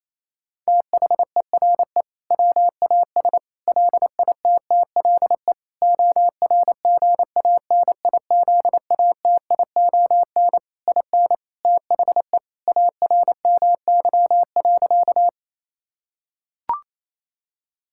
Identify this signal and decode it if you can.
Morse code, heard as a keyed tone, about 28 wpm, 700 Hz. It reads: THERE WAS LITTLE ORGANIZATION IN THE ARMY.
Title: Sentences from Top 1000 Words - Review Part 5 - 28wpm